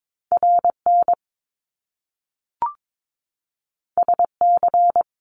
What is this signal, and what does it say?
Morse code, heard as a keyed tone, about 22 wpm, 700 Hz. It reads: RN SC
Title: Top Two Letter Combinations in Words 22wpm